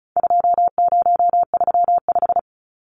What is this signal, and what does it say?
Morse code, heard as a keyed tone, about 34 wpm, 700 Hz. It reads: 2035
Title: Numbers - 4-Digit 35wpm